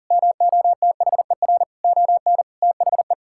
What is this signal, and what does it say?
Morse code, heard as a keyed tone, about 37 wpm, 700 Hz. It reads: MOTHER ON THE
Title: Sentences from Top 1000 Words - Review Part 6 - 40wpm